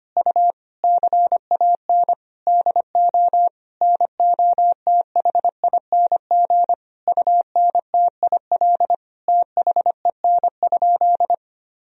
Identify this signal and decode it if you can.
Morse code, heard as a keyed tone, about 25 wpm, 700 Hz. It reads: U CAN DO NOTHING UNTIL THEN?
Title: Sentences from Top 500 Words and Abbreviations 25wpm